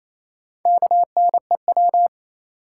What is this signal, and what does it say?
Morse code, heard as a keyed tone, about 28 wpm, 700 Hz. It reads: KNEW
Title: Sentences from Top 400 Words - New Word Drill - 28wpm